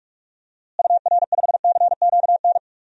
Morse code, heard as a keyed tone, about 42 wpm, 700 Hz. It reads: UR5CQN
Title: DXCC-100 Call Signs - 6 Characters - Repeated 3x - 45wpm